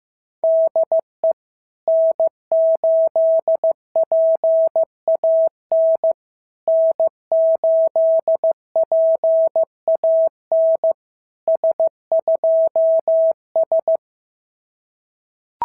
Morse code, heard as a keyed tone, about 15 wpm, 650 Hz. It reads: DE N8PAN N8PAN S2S
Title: SOTA QSO Practice 15wpm